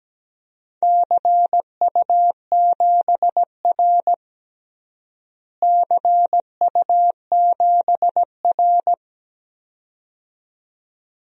Morse code, heard as a keyed tone, about 17 wpm, 700 Hz. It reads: CU7R CU7R
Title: DXCC-100 Call Signs - 4 Characters - Repeated 2x - 17wpm